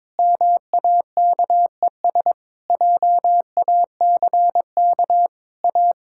Morse code, heard as a keyed tone, about 22 wpm, 700 Hz. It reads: MAKES JACK A
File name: practice / 22wpm / Sentences with 8-Word Phrases - 22wpm